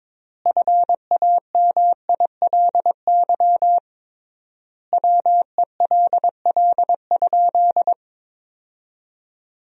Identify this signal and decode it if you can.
Morse code, heard as a keyed tone, about 22 wpm, 700 Hz. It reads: FAMILY WELL?
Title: 3x Word-Spacing - Sentences from Top 400 Words - 22wpm